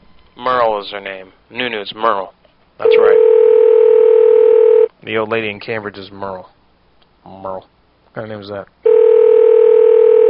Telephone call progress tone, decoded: ringback tone